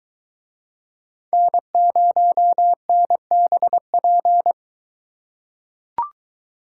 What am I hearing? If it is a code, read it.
Morse code, heard as a keyed tone, about 21 wpm, 700 Hz. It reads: N0NBP